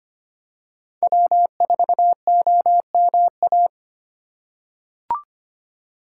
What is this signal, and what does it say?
Morse code, heard as a keyed tone, about 25 wpm, 700 Hz. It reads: W4OMA
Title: USA Call Signs - 8x Character Spacing - 25wpm